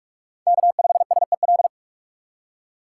Morse code, heard as a keyed tone, about 42 wpm, 700 Hz. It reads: K5SEL
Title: USA Call Signs - 5 Characters - 4x Character Spacing - 45wpm